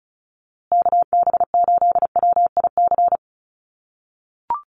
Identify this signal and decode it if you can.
Morse code, heard as a keyed tone, about 34 wpm, 700 Hz. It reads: KB8WIC
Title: USA Call Signs Repeated 2x 35wpm